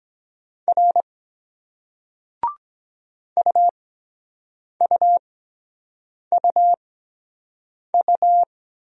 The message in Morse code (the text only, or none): R UUUU